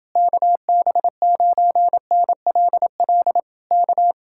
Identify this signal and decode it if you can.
Morse code, heard as a keyed tone, about 29 wpm, 700 Hz. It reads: KB9NLL K